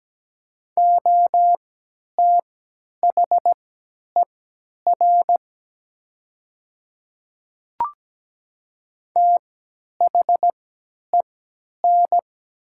Morse code, heard as a keyed tone, about 17 wpm, spaced as for 10 wpm, 700 Hz. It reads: OTHER THEN
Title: 3x Character Spacing - Top 200 Words - 17wpm